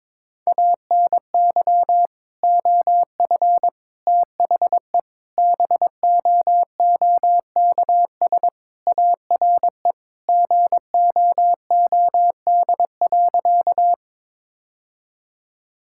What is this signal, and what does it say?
Morse code, heard as a keyed tone, about 23 wpm, 700 Hz. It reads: ANY OF THE BOOKS ARE GOOD.
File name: practice / 22wpm / Sentences with 3-Word Phrases - 22wpm